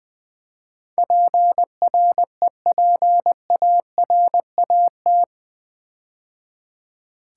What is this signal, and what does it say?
Morse code, heard as a keyed tone, about 20 wpm, 700 Hz. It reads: PREPARAT
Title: Build A Word - 4 Letters At A Time - 11-Letter Words - 20wpm